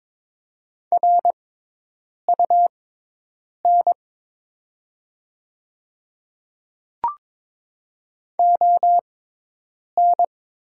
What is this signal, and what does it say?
Morse code, heard as a keyed tone, about 22 wpm, spaced as for 8 wpm, 700 Hz. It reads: RUN ON
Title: 6x Character Spacing - Top 300 Words - 22wpm